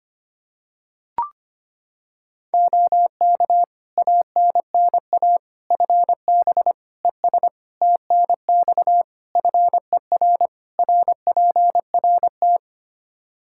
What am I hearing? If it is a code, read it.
Morse code, heard as a keyed tone, about 25 wpm, 700 Hz. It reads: OK ANNA FB ES TNX FER RPRT
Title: Practice QSO - Exchange 2 - Info - 25wpm